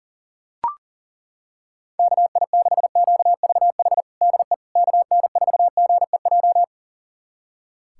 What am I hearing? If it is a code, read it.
Morse code, heard as a keyed tone, about 40 wpm, 700 Hz. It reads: KI6QVH DE KN4GEJ
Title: USA Callsign DE Callsign 40wpm